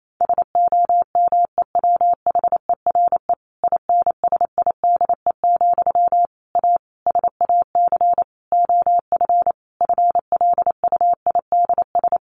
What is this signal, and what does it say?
Morse code, heard as a keyed tone, about 28 wpm, 700 Hz. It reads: SOMEWHERE INSIDE, A SAC OF FLUIDS
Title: A Sound Of Thunder 28wpm